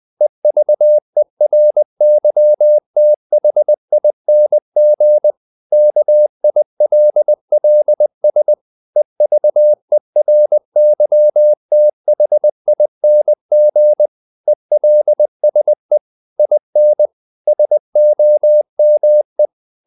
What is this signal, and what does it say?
Morse code, heard as a keyed tone, about 20 wpm, 600 Hz. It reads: EVERYTHING KILLS EVERYTHING ELSE IN SOME
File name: the old man and the sea part 2 20wpm